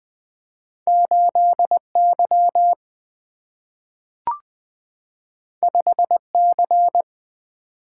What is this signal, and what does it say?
Morse code, heard as a keyed tone, about 20 wpm, 700 Hz. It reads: 8Y 5C
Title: Single-Letter Single-Number 20wpm